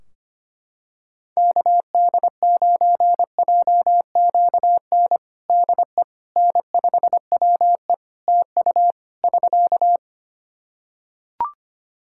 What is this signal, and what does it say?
Morse code, heard as a keyed tone, about 25 wpm, 700 Hz. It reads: KD9JQN DE N5WE TU <SK>
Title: Practice QSO - Exchange 4 - Ending - 25wpm